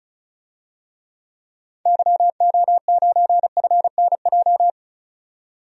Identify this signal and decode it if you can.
Morse code, heard as a keyed tone, about 34 wpm, 700 Hz. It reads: YO9FNJ
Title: DXCC-100 Call Signs - 6 Characters - 35wpm